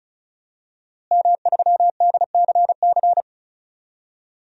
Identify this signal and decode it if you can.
Morse code, heard as a keyed tone, about 34 wpm, 690 Hz. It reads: M3DCC
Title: DXCC-100 Call Signs - 5 Characters - Repeated 3x - 35wpm